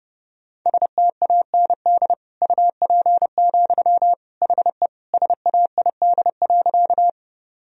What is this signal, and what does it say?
Morse code, heard as a keyed tone, about 30 wpm, 700 Hz. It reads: STAND UP, HE SAID.
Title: Sentences from Top 1000 Words - Review Part 1 - 30wpm